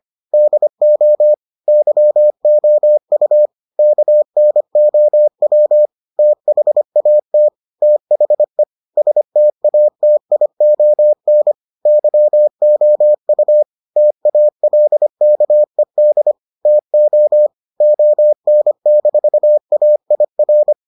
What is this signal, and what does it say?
Morse code, heard as a keyed tone, about 25 wpm, 600 Hz. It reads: DO YOU KNOW THAT THE STATION YOU TALKED TO ON-AIR